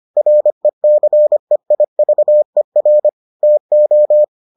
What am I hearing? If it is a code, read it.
Morse code, heard as a keyed tone, about 25 wpm, 600 Hz. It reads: RECEIVER TO